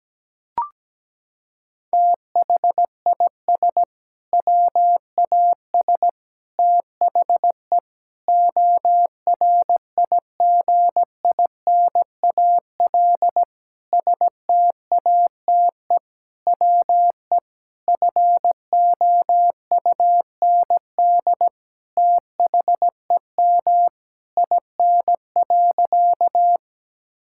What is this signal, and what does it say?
Morse code, heard as a keyed tone, about 17 wpm, 700 Hz. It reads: THIS WAS THE ORIGINAL STATE WE FOUND THEM IN.
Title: Sentences from Top 1000 Words - Review Part 6 - 17wpm